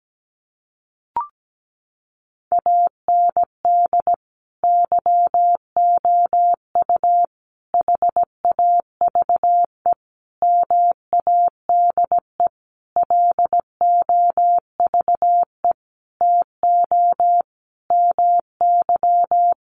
Morse code, heard as a keyed tone, about 17 wpm, 700 Hz. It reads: AND YOU HAVE MADE LOVE TO MY
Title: Sentences from Top 1500 Words 17wpm